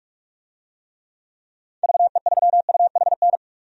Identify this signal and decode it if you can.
Morse code, heard as a keyed tone, about 46 wpm, 700 Hz. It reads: VE3UHN
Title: DXCC-100 Call Signs - 6 Characters - Repeated 2x - 45wpm